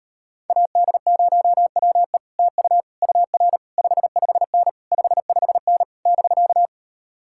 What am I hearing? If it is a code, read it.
Morse code, heard as a keyed tone, about 33 wpm, 700 Hz. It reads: AD0WE TU UR 55N 55N <BK>